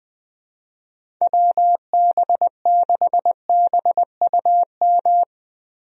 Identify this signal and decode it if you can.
Morse code, heard as a keyed tone, about 20 wpm, 700 Hz. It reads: WB6BUM
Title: USA Call Signs - 6 Characters - Repeated 2x - 20wpm